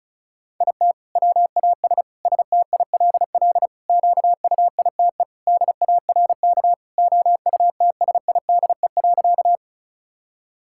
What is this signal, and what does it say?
Morse code, heard as a keyed tone, about 34 wpm, 700 Hz. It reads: IT WAS STILL QUITE DARK OUTSIDE.
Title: Sentences from Top 1000 Words - Review Part 6 - 35wpm